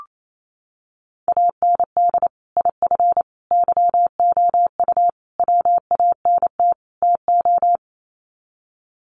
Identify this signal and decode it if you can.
Morse code, heard as a keyed tone, about 28 wpm, 700 Hz. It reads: AND IF YOU WANT TO
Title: Sets of 5 Words Encore 28wpm